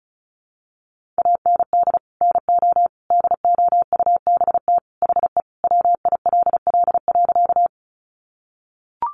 Morse code, heard as a keyed tone, about 34 wpm, 700 Hz. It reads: AND NO DOUBT HE WILL.
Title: Sentences from Top 1500 Words 35wpm